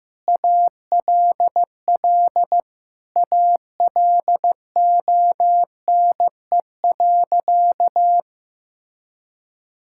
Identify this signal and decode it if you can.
Morse code, heard as a keyed tone, about 15 wpm, 700 Hz. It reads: ALL ALONE.
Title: Sentences from Top 1000 Words - Review Part 5 - 15wpm